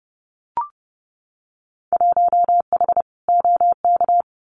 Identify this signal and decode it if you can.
Morse code, heard as a keyed tone, about 30 wpm, 700 Hz. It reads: 1H OK
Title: Winter Field Day Exchanges 30wpm